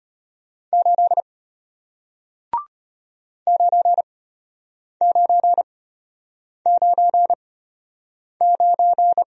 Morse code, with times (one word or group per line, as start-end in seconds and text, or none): 0.7-1.2 8
3.5-9.2 9999